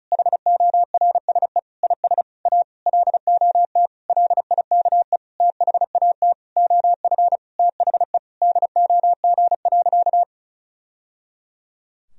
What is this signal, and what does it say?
Morse code, heard as a keyed tone, about 34 wpm, 700 Hz. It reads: HORSE IS A LOT LIKE THAT OF THE DOG.
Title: Sentences from Top 1000 Words - Review Part 2 - 35wpm